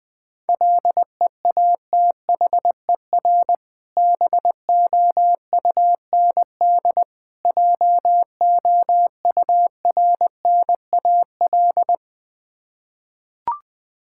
Morse code, heard as a keyed tone, about 20 wpm, 700 Hz. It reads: LEATHER BOUND JOURNAL
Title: Difficult Words with Hints - 20wpm